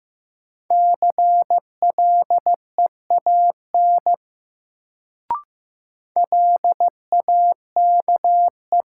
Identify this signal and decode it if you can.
Morse code, heard as a keyed tone, about 15 wpm, 700 Hz. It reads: CLEAN LAKE
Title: Sentences from Top 1100 Words - New Word Drill Alt - 15wpm